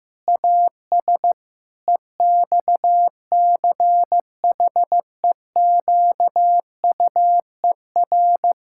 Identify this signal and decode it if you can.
Morse code, heard as a keyed tone, about 16 wpm, 700 Hz. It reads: AS EXCHEQUER
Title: Pangram Challenge v2 - 15wpm